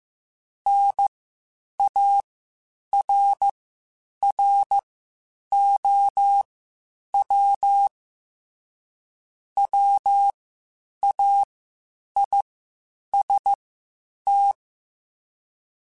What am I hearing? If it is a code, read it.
Morse code, heard as a keyed tone, about 15 wpm, spaced as for 8 wpm, 800 Hz. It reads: NARROW WAIST
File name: Spaced / Chuckles Spaced-mp3s / MiddleAge sp @13